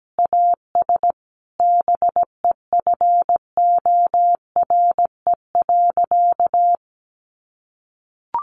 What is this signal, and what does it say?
Morse code, heard as a keyed tone, about 17 wpm, 700 Hz. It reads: AS BEFORE.
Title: Sentences from Top 1400 Words 17wpm